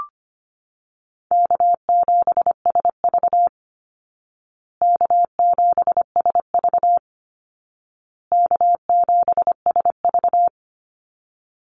Morse code, heard as a keyed tone, about 25 wpm, 700 Hz. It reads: K7SV K7SV K7SV
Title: Easy Call Signs Repeated 3x - 25wpm